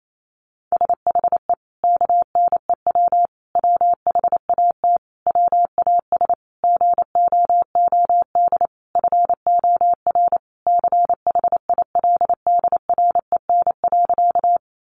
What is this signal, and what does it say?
Morse code, heard as a keyed tone, about 28 wpm, 700 Hz. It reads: SHE KNEW WHAT WAS GOOD FOR CHILDREN.
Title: Sentences from Top 400 Words 28wpm